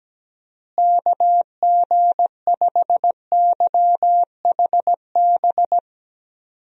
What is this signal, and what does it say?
Morse code, heard as a keyed tone, about 17 wpm, 700 Hz. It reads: KG5YHB
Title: USA Call Signs - 6 Characters - Repeated 2x - 17wpm